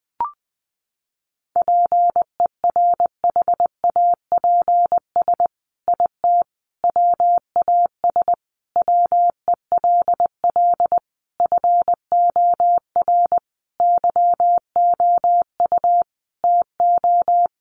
Morse code, heard as a keyed tone, about 20 wpm, 700 Hz. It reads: PERHAPS IT WAS WELL FOR YOU TO
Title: Sentences from Top 1400 Words 20wpm